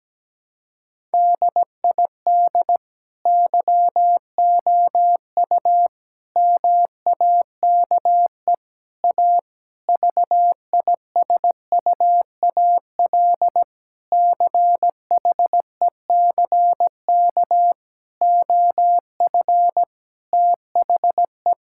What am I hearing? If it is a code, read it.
Morse code, heard as a keyed tone, about 17 wpm, 700 Hz. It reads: DID YOU MAKE A VISUAL CHECK OF THE
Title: Sentences from Top 1000 Words - Review Part 6 - 17wpm